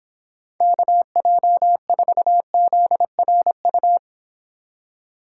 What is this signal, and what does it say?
Morse code, heard as a keyed tone, about 29 wpm, 700 Hz. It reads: KJ4ZRU